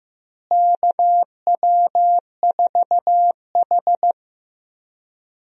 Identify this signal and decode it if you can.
Morse code, heard as a keyed tone, about 15 wpm, 700 Hz. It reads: KW4H
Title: USA Call Signs - 4 Characters - Repeated 3x - 15wpm